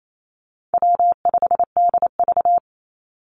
Morse code, heard as a keyed tone, about 28 wpm, 700 Hz. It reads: W5DV
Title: Easy Call Signs - 28wpm